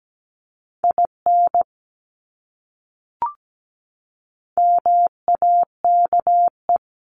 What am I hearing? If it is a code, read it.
Morse code, heard as a keyed tone, about 17 wpm, 700 Hz. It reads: IN MAKE
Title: Top 300 Words Repeated 17wpm